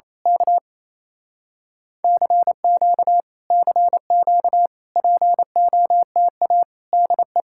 Morse code, heard as a keyed tone, about 26 wpm, 700 Hz. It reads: K CQ CQ POTA DE
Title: Speed-Racing Common QSO Phrases - 28wpm